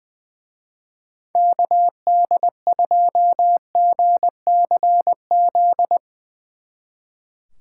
Morse code, heard as a keyed tone, about 24 wpm, 700 Hz. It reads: KD2GCZ
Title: Speed Racing - USA 2x3 Call Signs 20wpm